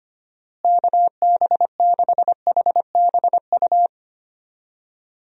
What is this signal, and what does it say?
Morse code, heard as a keyed tone, about 25 wpm, 700 Hz. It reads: KB6HBU
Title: USA Call Signs - 6 Characters - Repeated 2x - 25wpm